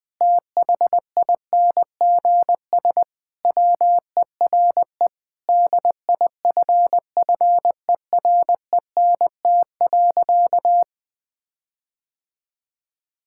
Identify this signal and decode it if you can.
Morse code, heard as a keyed tone, about 21 wpm, 700 Hz. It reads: THINGS WERE DIFFERENT.
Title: Sentences with 6-Word Phrases - 20wpm